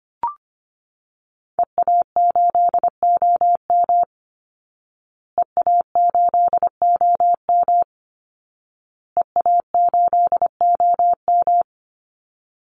Morse code, heard as a keyed tone, about 25 wpm, 700 Hz. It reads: EA8OM EA8OM EA8OM
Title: Call Signs Repeated 2x - 25wpm